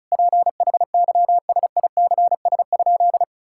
Morse code, heard as a keyed tone, about 36 wpm, 700 Hz. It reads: PHYSICS?
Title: Sentences from Top 1000 Words - Review Part 6 - 35wpm